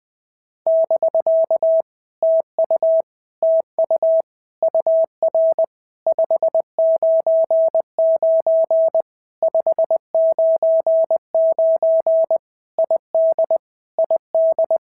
Morse code, heard as a keyed tone, about 20 wpm, 650 Hz. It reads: <BK> TU TU UR 599 599 ID ID
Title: POTA QSO Practice - 20wpm